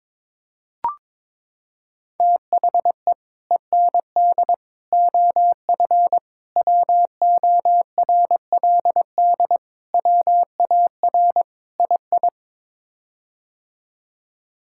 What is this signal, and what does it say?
Morse code, heard as a keyed tone, about 22 wpm, 700 Hz. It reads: THE END OF WORLD WAR II
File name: practice / 22wpm / Sets of 6 Words - 22wpm